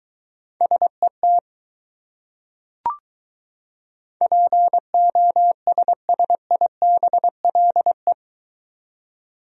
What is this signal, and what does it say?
Morse code, heard as a keyed tone, about 22 wpm, 700 Hz. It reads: SET POSSIBLE